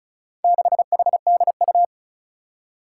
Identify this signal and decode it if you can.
Morse code, heard as a keyed tone, about 36 wpm, 700 Hz. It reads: 6HDU
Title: Four Challenging Letter-Number Combinations 35wpm